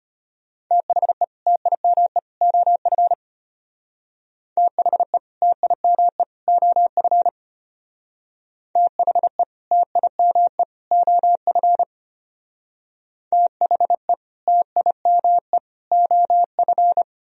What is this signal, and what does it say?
Morse code, heard as a keyed tone, about 33 wpm, 700 Hz. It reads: THE TIME OF THE TIME OF THE TIME OF THE TIME OF